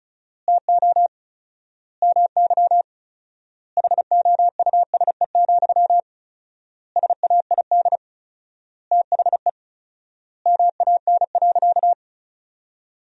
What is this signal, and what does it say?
Morse code, heard as a keyed tone, about 34 wpm, 700 Hz. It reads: TO MY HOUSE, SAID THE MAN.
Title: 4x Word-Spacing - Sentences from Top 400 Words - 35wpm